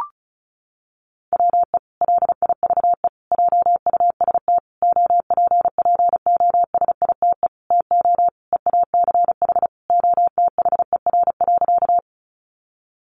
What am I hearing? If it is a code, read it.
Morse code, heard as a keyed tone, about 34 wpm, 700 Hz. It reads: WE LIVE JUST OPPOSITE TO EACH OTHER.
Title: Sentences from Top 1300 Words 35wpm